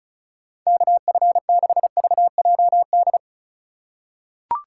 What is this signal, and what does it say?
Morse code, heard as a keyed tone, about 34 wpm, 700 Hz. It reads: KF6VJD
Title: USA Call Signs - 6 Characters - 2x Character Spacing - 35wpm